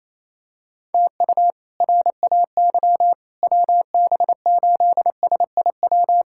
Morse code, heard as a keyed tone, about 28 wpm, 700 Hz. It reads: TU RAY WB8SIW
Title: K1USN SST QSO Practice - 28wpm